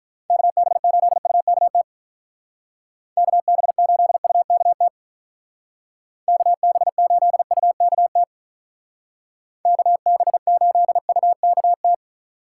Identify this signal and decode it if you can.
Morse code, heard as a keyed tone, about 43 wpm, 700 Hz. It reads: KB8UKT KB8UKT KB8UKT KB8UKT
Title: Speed Racing - USA 2x3 Call Signs 35wpm